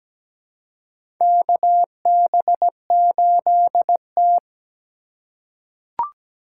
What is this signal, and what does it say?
Morse code, heard as a keyed tone, about 17 wpm, 700 Hz. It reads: KB8T
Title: USA Call Signs - 2x Character Spacing - 17wpm